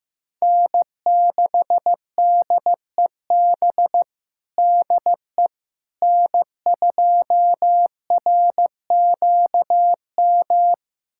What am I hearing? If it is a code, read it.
Morse code, heard as a keyed tone, about 15 wpm, 700 Hz. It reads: N6DEB DE N2RQM
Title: USA Callsign DE Callsign - 5 Characters - 15wpm